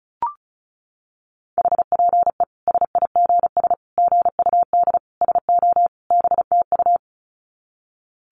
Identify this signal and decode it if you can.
Morse code, heard as a keyed tone, about 34 wpm, 700 Hz. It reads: HPE SIGS GUD SO BTU
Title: Common QSO Phrases 35wpm